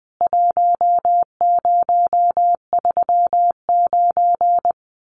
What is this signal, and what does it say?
Morse code, heard as a keyed tone, about 20 wpm, 700 Hz. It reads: 1039
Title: CWT Exchange 20wpm